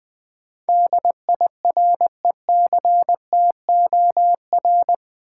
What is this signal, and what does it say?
Morse code, heard as a keyed tone, about 20 wpm, 700 Hz. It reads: DIRECTOR
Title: Sentences from Top 1300 Words - New Word Drill Alt - 20wpm